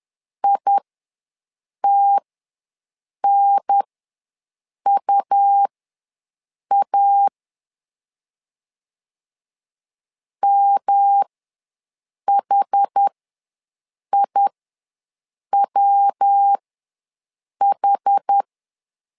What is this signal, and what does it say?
Morse code, heard as a keyed tone, about 10 wpm, spaced as for 6 wpm, 800 Hz. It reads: ITNUA MHIWH
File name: LEZIONI / 3 MWRO / 3 MW ws 10wpm 2213